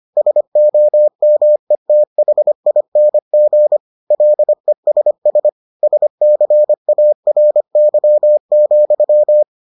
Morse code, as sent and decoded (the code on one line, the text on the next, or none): ... --- -- . - .... .. -. --. / .-.. ... ... / ... -.-. .- .-. -.-- --..--
SOMETHING LESS SCARY,